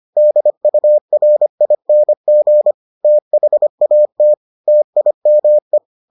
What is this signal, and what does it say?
Morse code, heard as a keyed tone, about 25 wpm, 600 Hz. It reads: DURING THAT TIME